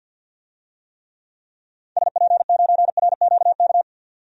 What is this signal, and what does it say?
Morse code, heard as a keyed tone, about 52 wpm, 700 Hz. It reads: SP9RQK
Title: DXCC-100 Call Signs - 6 Characters - 50wpm